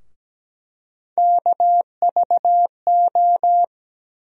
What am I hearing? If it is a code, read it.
Morse code, heard as a keyed tone, about 17 wpm, 700 Hz. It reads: KVO
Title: USA Call Sign Postfixes - 3 Letters 17wpm